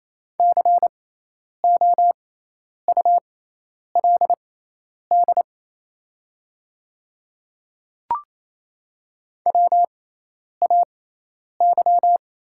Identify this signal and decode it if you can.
Morse code, heard as a keyed tone, about 28 wpm, spaced as for 10 wpm, 690 Hz. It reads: COULD WAY
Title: 6x Character Spacing - Top 200 Words - 28wpm